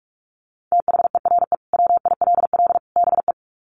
Morse code, heard as a keyed tone, about 44 wpm, 700 Hz. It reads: THERE WILL BE
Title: Sets of 3 Words 45wpm